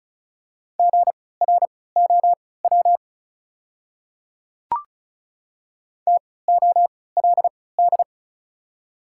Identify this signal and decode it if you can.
Morse code, heard as a keyed tone, about 36 wpm, spaced as for 20 wpm, 690 Hz. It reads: GROW TOLD
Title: 3x Character Spacing - Top 500 Words - 35wpm